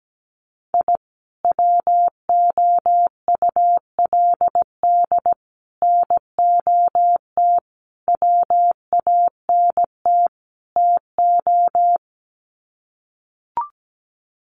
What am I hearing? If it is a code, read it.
Morse code, heard as a keyed tone, about 17 wpm, 700 Hz. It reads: I WOULD NOT WANT TO
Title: Sets of 5 Words 17wpm